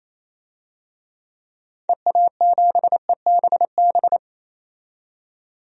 Morse code, heard as a keyed tone, about 28 wpm, 700 Hz. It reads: EA7EBB
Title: DXCC-100 Call Signs - 6 Characters - 28wpm